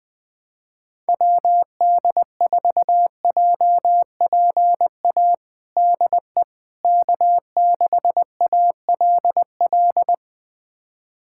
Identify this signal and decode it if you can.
Morse code, heard as a keyed tone, about 19 wpm, 700 Hz. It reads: WD4JPA DE K6ALL
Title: USA Callsign DE Callsign 20wpm